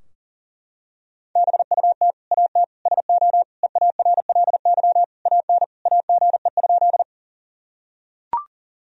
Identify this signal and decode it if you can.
Morse code, heard as a keyed tone, about 40 wpm, 700 Hz. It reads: BUT AT SO EARLY AN AGE?
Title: Sentences from Top 1000 Words - Review Part 3 - 40wpm